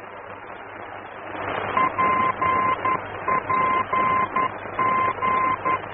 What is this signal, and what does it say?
Morse code, heard as a keyed tone, about 11 wpm, 1040 Hz, with noise. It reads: PPG
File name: france / mx / PPG-F-VOR